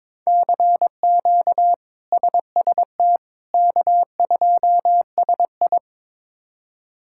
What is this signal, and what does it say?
Morse code, heard as a keyed tone, about 22 wpm, 700 Hz. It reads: CQ SST K2SI
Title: K1USN SST QSO Practice - 22wpm